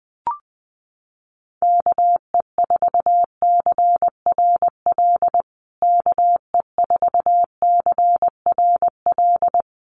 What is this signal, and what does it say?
Morse code, heard as a keyed tone, about 20 wpm, 700 Hz. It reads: KE4CRL KE4CRL
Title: USA Call Signs Repeated 3x 20wpm